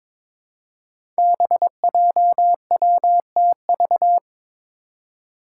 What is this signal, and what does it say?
Morse code, heard as a keyed tone, about 22 wpm, 700 Hz. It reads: BJWTV
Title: Five Challenging Letter-Number Combinations 22wpm